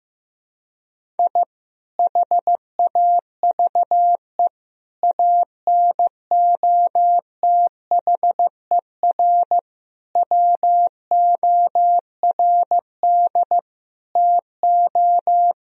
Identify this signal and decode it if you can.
Morse code, heard as a keyed tone, about 16 wpm, 700 Hz. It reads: I HAVE ANOTHER WORD TO
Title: Sentences from Top 1000 Words - Review Part 1 - 15wpm